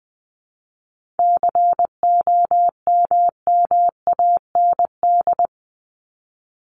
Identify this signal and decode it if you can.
Morse code, heard as a keyed tone, about 20 wpm, 700 Hz. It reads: COMMAND
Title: Top 1501-1750 Words 20wpm